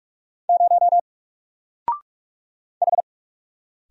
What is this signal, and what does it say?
Morse code, heard as a keyed tone, about 45 wpm, 700 Hz. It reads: 0 H
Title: Single Letter-Number Alt 45wpm